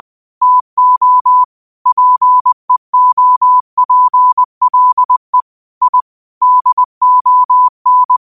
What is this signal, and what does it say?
Morse code, heard as a keyed tone, about 20 wpm, 1000 Hz. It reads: TO PEOPLE I DON